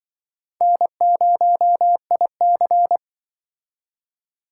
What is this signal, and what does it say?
Morse code, heard as a keyed tone, about 25 wpm, 700 Hz. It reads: N0IC
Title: Speed Racing - USA 1x2 Call Signs 20wpm